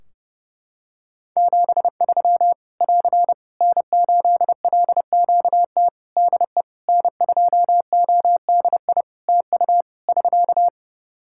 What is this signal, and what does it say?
Morse code, heard as a keyed tone, about 30 wpm, 700 Hz. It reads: 73 <AR> N8LQT DE N2ODI TU <SK>
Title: K1USN SST QSO Practice 30wpm